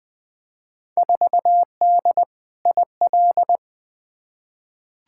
Morse code, heard as a keyed tone, about 21 wpm, 700 Hz. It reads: 4D IL
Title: Field Day Exchanges Encore - 20wpm